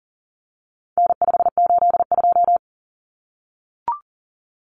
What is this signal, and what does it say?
Morse code, heard as a keyed tone, about 40 wpm, 700 Hz. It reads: N582
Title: Common Cut Numbers - 4-Digit 40wpm